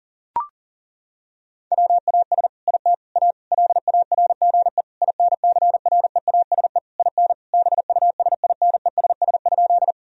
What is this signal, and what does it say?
Morse code, heard as a keyed tone, about 40 wpm, 700 Hz. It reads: WAS IT A LARGE INCREASE IN BUSINESS?
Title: Sentences from Top 1000 Words - Review Part 4 - 40wpm